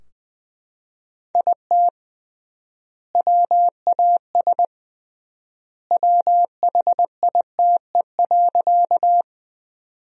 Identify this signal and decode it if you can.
Morse code, heard as a keyed tone, about 20 wpm, 700 Hz. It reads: IT WAS WHITE.
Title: 3x Word-Spacing - Sentences from Top 300 Words - 20wpm